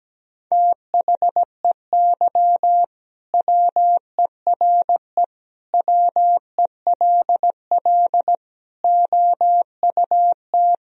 Morse code, heard as a keyed tone, about 17 wpm, 700 Hz. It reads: THEY WERE WELL OUT
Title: Sentences from Top 1000 Words - Review Part 3 - 17wpm